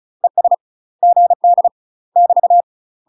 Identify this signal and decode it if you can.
Morse code, heard as a keyed tone, about 36 wpm, 700 Hz. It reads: ES GD <BT>
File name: UK MASTER QSOs TRAINING / 35WPM / UK MASTER QSOs 0033